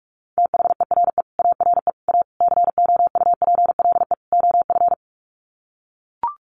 Morse code, heard as a keyed tone, about 44 wpm, 700 Hz. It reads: THERE ARE A COUPLE OF
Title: Sets of 5 Words Encore 45wpm